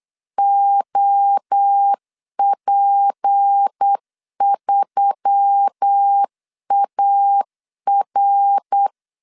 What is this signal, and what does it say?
Morse code, heard as a keyed tone, about 8 wpm, 800 Hz. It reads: OP3AR
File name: CW / DET / 1 6 1500C S C 8wpm 5219